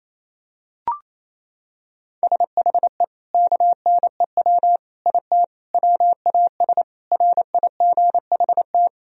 Morse code, heard as a keyed tone, about 28 wpm, 700 Hz. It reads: SHE KNEW IT WAS RIGHT
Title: Sentences from Top 1000 Words - Review Part 2 - 28wpm